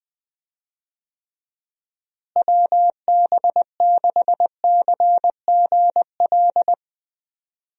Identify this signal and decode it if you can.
Morse code, heard as a keyed tone, about 21 wpm, 700 Hz. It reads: WB6CGL
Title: USA Call Signs - 6 Characters - Repeated 2x - 20wpm